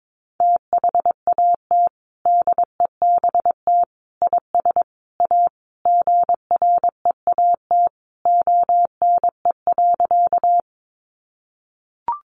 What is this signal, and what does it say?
Morse code, heard as a keyed tone, about 22 wpm, 700 Hz. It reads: THAT DEBT IS A GREAT ONE.
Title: Sentences from Top 1300 Words 22wpm